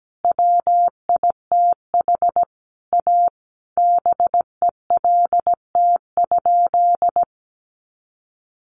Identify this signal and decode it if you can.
Morse code, heard as a keyed tone, about 17 wpm, 700 Hz. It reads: WITH A BELT?
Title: Sentences from Top 1400 Words 17wpm